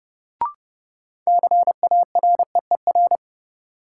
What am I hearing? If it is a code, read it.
Morse code, heard as a keyed tone, about 30 wpm, 700 Hz. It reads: CAREER
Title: Sentences from Top 1000 Words - New Word Drill - 30wpm